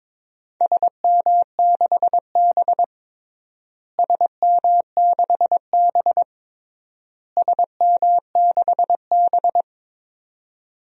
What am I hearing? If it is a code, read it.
Morse code, heard as a keyed tone, about 22 wpm, 700 Hz. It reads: SM6B SM6B SM6B
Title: DXCC-100 Call Signs - 4 Characters - Repeated 3x - 22wpm